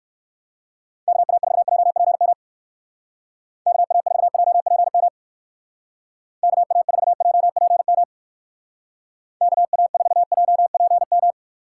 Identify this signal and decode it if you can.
Morse code, heard as a keyed tone, about 59 wpm, 700 Hz. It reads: KA4JPM KA4JPM KA4JPM KA4JPM